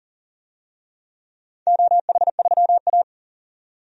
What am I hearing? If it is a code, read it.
Morse code, heard as a keyed tone, about 40 wpm, 700 Hz. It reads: OH3A
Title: DXCC-100 Call Signs - 4 Characters - 40wpm